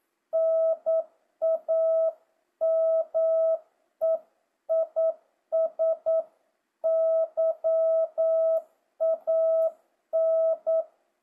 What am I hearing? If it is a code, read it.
Morse code, heard as a keyed tone, about 9 wpm, 650 Hz. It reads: NAME IS YAN
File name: jh8cbh / 2025cw